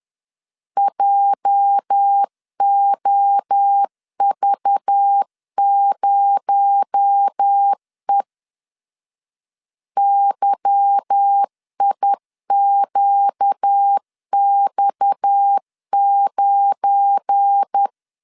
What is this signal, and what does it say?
Morse code, heard as a keyed tone, about 10 wpm, 800 Hz. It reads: JOV0E YIQX9
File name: LEZIONI / 8 YJ45 / 8 YJ4 10wpm 2633